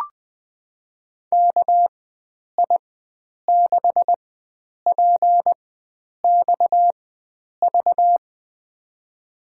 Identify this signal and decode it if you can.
Morse code, heard as a keyed tone, about 20 wpm, spaced as for 9 wpm, 700 Hz. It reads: KI6PXV
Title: USA Call Signs - 6 Characters - 4x Character Spacing - 20wpm